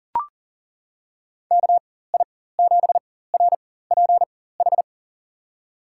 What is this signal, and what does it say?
Morse code, heard as a keyed tone, about 40 wpm, spaced as for 19 wpm, 700 Hz. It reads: KI7RPH
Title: USA Call Signs - 4x Character Spacing - 40wpm